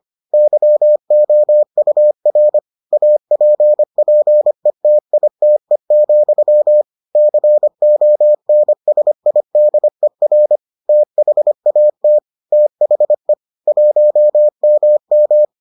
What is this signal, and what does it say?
Morse code, heard as a keyed tone, about 25 wpm, 600 Hz. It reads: YOUR APPETITE, CONSIDER THAT THE 1MM